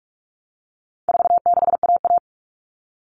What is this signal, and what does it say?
Morse code, heard as a keyed tone, about 44 wpm, 700 Hz. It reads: 46AA
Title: All Cut Numbers - 4-Digit 45wpm